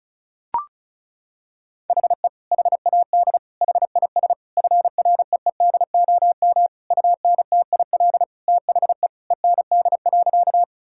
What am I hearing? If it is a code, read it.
Morse code, heard as a keyed tone, about 34 wpm, 700 Hz. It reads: HE HAD HIS FREEDOM UNTIL THE END.
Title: Sentences from Top 1000 Words - Review Part 5 - 35wpm